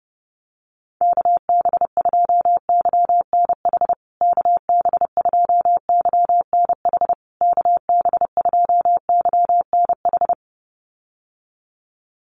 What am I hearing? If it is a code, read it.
Morse code, heard as a keyed tone, about 30 wpm, 700 Hz. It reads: KB2YNH KB2YNH KB2YNH
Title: USA Call Signs Repeated 3x 30wpm